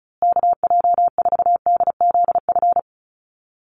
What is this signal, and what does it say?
Morse code, heard as a keyed tone, about 34 wpm, 700 Hz. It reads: KJ4DZF
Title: USA Call Signs Repeated 2x 35wpm